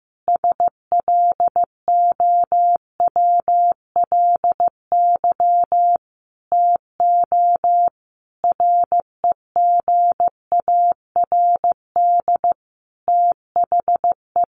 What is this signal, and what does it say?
Morse code, heard as a keyed tone, about 15 wpm, 700 Hz. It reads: SLOWLY TO REGARD THE
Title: A Sound Of Thunder 15wpm